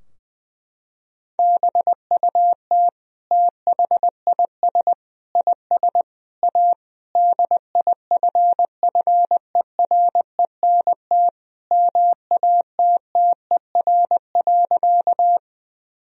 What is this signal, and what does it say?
Morse code, heard as a keyed tone, about 20 wpm, 700 Hz. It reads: BUT THIS IS A DIFFERENT MATTER.
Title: Sentences from Top 1000 Words - Review Part 3 - 20wpm